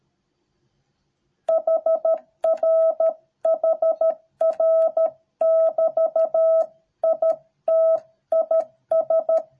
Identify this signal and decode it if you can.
Morse code, heard as a keyed tone, about 13 wpm, 650 Hz. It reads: HR HR <BT> IT IS